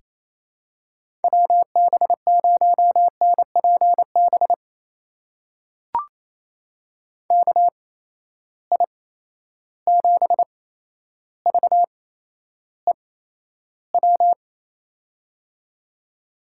Morse code, heard as a keyed tone, about 28 wpm, spaced as for 12 wpm, 700 Hz. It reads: WB0NPB KI7VEW
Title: USA Call Signs - 6 Characters - 8x Character Spacing - 28wpm